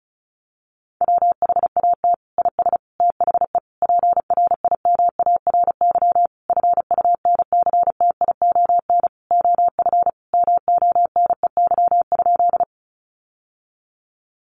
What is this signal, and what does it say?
Morse code, heard as a keyed tone, about 34 wpm, 700 Hz. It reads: WHAT IS THE PRIMARY FUNCTION OF MONEY?
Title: Sentences from Top 1100 Words 35wpm